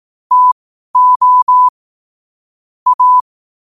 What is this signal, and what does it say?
Morse code, heard as a keyed tone, about 18 wpm, spaced as for 12 wpm, 1000 Hz. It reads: TO A
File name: qotdfiles / 12wpm